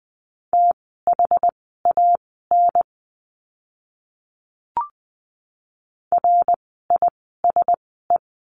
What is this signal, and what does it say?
Morse code, heard as a keyed tone, about 20 wpm, spaced as for 15 wpm, 700 Hz. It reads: THAN RISE
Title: 061 Lesson 11f - Letter h - Words 20wpm